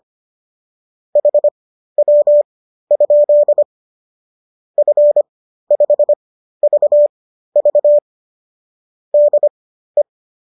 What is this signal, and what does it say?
Morse code, heard as a keyed tone, about 25 wpm, spaced as for 13 wpm, 570 Hz. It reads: HW? F5VV DE
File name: qso214 13